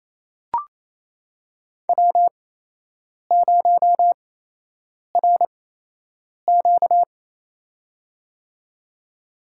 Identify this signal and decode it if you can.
Morse code, heard as a keyed tone, about 28 wpm, spaced as for 8 wpm, 700 Hz. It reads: W0RQ
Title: USA Call Signs - 8x Character Spacing - 28wpm